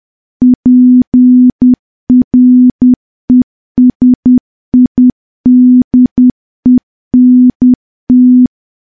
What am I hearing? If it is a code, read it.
Morse code, heard as a keyed tone, about 10 wpm, 250 Hz. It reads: PRESIDENT
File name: morse / morse-training-10WPM